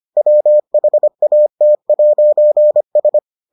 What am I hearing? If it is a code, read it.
Morse code, heard as a keyed tone, about 25 wpm, 600 Hz. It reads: WHAT'S